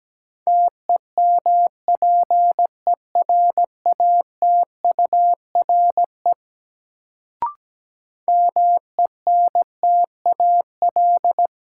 Morse code, heard as a keyed tone, about 17 wpm, 700 Hz. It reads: TEMPERATURE MENTAL
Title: Sentences from Top 1000 Words - Review Part 4 - New Word Drill - 17wpm